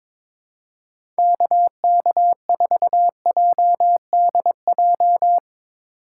Morse code, heard as a keyed tone, about 22 wpm, 700 Hz. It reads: KK4JDJ
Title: USA Call Signs - 6 Characters - 3x Character Spacing - 22wpm